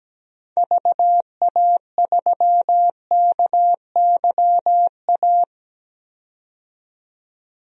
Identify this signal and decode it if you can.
Morse code, heard as a keyed tone, about 17 wpm, 700 Hz. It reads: VA3KYA
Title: DXCC-100 Call Signs - 6 Characters - 17wpm